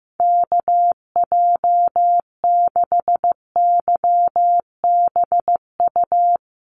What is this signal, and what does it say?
Morse code, heard as a keyed tone, about 15 wpm, 700 Hz. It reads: KJ6YBU
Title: USA Call Signs Repeated 2x 15wpm